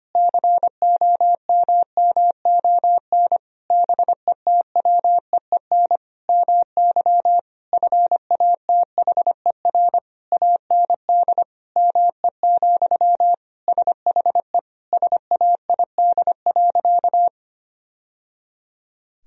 Morse code, heard as a keyed tone, about 25 wpm, 700 Hz. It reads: COMMON BETWEEN MY FATHER AND ME, SHE SAID.
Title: Sentences from Top 1000 Words - Review Part 3 - 25wpm